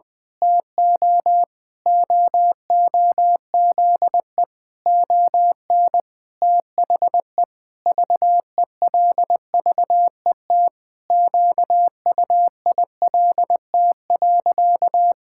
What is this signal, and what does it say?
Morse code, heard as a keyed tone, about 20 wpm, 700 Hz. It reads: TO OOZE ON THE VELVET QUILT.
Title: Pangram Challenge v2 - 20wpm